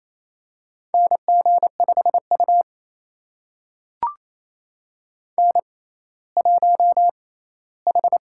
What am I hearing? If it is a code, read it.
Morse code, heard as a keyed tone, about 28 wpm, spaced as for 16 wpm, 700 Hz. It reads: NG5U N1H
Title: USA Call Signs - 6x Character Spacing - 28wpm